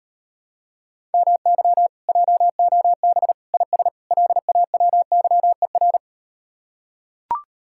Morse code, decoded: MY JOB IS LAWYER

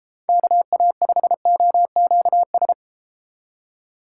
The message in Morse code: KA5OQS

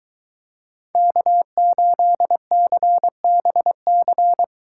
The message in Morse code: K8CBC

